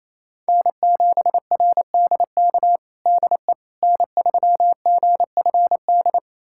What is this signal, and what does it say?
Morse code, heard as a keyed tone, about 28 wpm, 700 Hz. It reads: N7RDK DE N3GFD
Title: USA Callsign DE Callsign - 5 Characters - 28wpm